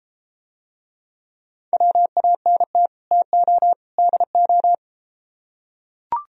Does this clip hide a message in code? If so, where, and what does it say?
Yes: Morse code from 1.7 to 4.7 s, heard as a keyed tone, about 34 wpm, 700 Hz. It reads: WANT TO DO